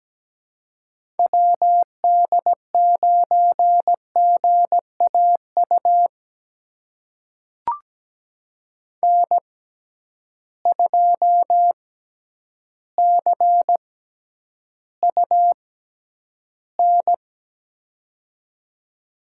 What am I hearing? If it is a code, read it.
Morse code, heard as a keyed tone, about 17 wpm, spaced as for 9 wpm, 700 Hz. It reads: WD9GAU N2CUN
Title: USA Call Signs - 6x Character Spacing - 17wpm